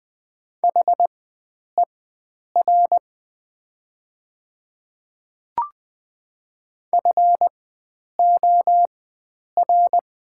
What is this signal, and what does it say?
Morse code, heard as a keyed tone, about 21 wpm, spaced as for 9 wpm, 700 Hz. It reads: HER FOR
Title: 4x Character Spacing - Top 100 Words - 20wpm